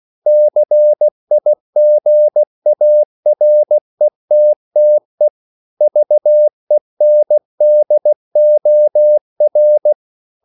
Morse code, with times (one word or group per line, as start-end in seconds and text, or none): 0.3-5.3 CIGARETTE
5.8-9.9 VENDOR